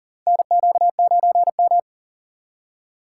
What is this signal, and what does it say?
Morse code, heard as a keyed tone, about 40 wpm, 700 Hz. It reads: NQ9M
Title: USA Call Signs - 4 Characters - 3x Character Spacing - 40wpm